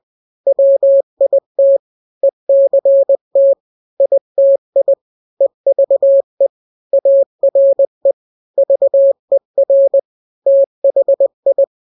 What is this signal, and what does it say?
Morse code, heard as a keyed tone, about 20 wpm, 550 Hz. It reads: WIT ECT ITI EVE ARE VER THI
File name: n-grams / 3-grams 20 18 WPM